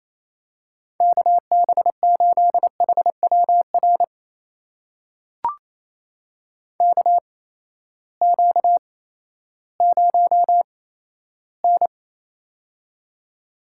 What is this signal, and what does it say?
Morse code, heard as a keyed tone, about 28 wpm, spaced as for 14 wpm, 700 Hz. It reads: KB8HWR KQ0N
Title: USA Call Signs - 8x Character Spacing - 28wpm